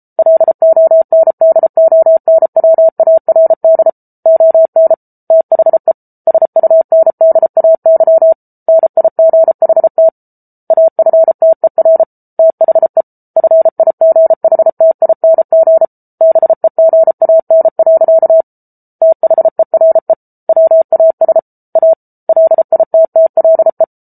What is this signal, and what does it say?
Morse code, heard as a keyed tone, about 34 wpm, 650 Hz. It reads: LONDONWARD ON THE SUNDAY NIGHT AFTER THE FIGHTING BEGAN. THERE WAS A LITTLE